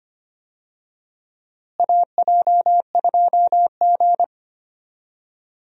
Morse code, heard as a keyed tone, about 25 wpm, 700 Hz. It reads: AJ2G
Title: USA Call Signs - 4 Characters - Repeated 2x - 25wpm